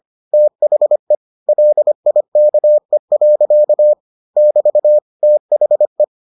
Morse code, heard as a keyed tone, about 25 wpm, 600 Hz. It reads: THE LIKE. <BT> THE